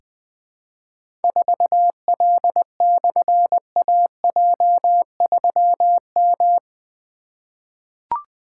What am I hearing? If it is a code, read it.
Morse code, heard as a keyed tone, about 20 wpm, 700 Hz. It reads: 4L/AJ3M
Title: Difficult Call Signs 20wpm